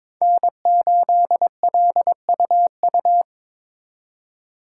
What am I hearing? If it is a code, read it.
Morse code, heard as a keyed tone, about 26 wpm, 700 Hz. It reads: N8LUU
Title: Speed Racing - USA 1x3 Call Signs 22wpm